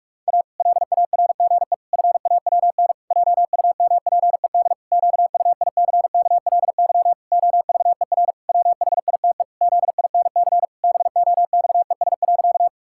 Morse code, heard as a keyed tone, about 46 wpm, 700 Hz. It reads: A LARGE FAWN JUMPED QUICKLY OVER WHITE ZINC BOXES.
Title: Pangram Challenge v2 - 45wpm